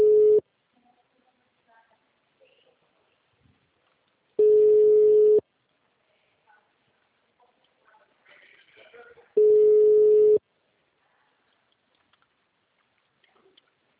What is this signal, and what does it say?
Ringback tone (Europe, 425 Hz).